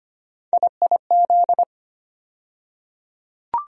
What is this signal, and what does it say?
Morse code, heard as a keyed tone, about 25 wpm, 700 Hz. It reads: IIZ
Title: USA Call Sign Postfixes - 3 Letters 25wpm